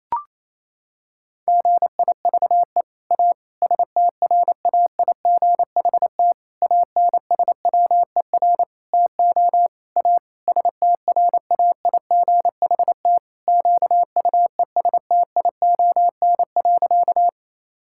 Morse code, heard as a keyed tone, about 28 wpm, 700 Hz. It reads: GIVE A STRAIGHT ANSWER TO A STRAIGHT QUESTION.
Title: Sentences from Top 1000 Words - Review Part 6 - 28wpm